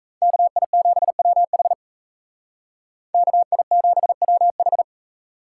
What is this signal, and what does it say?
Morse code, heard as a keyed tone, about 36 wpm, 700 Hz. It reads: KI7WH KI7WH